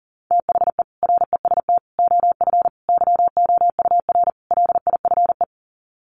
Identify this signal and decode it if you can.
Morse code, heard as a keyed tone, about 40 wpm, 700 Hz. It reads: THE REST OF YOUR LIFE
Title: Sets of 5 Words 40wpm